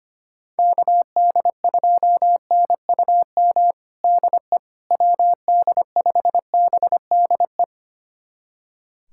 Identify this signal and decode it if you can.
Morse code, heard as a keyed tone, about 25 wpm, 700 Hz. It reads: KD2NUM DE WD5BDE
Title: USA Callsign DE Callsign 25wpm